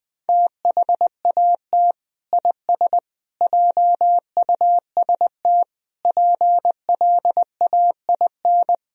Morse code, heard as a keyed tone, about 20 wpm, 700 Hz. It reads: THAT IS JUST PLAIN
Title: Sentences from Top 1000 Words - Review Part 4 - 20wpm